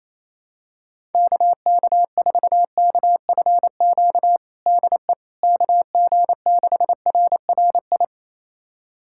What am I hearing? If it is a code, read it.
Morse code, heard as a keyed tone, about 28 wpm, 700 Hz. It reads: KK4KFQ DE KG6RRI
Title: USA Callsign DE Callsign - 6 Characters - 28wpm